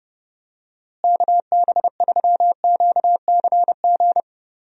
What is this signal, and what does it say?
Morse code, heard as a keyed tone, about 31 wpm, 700 Hz. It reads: KB3QCG
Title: USA Call Signs - 6 Characters - Repeated 3x - 30wpm